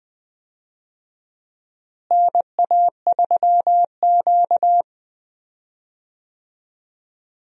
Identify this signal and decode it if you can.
Morse code, heard as a keyed tone, about 20 wpm, 690 Hz. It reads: NA3Q
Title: USA Call Signs - 4 Characters - Repeated 3x - 20wpm